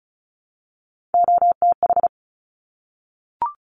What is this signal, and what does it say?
Morse code, heard as a keyed tone, about 36 wpm, 690 Hz. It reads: OTH
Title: Top Three Letter Combinations in Words 35wpm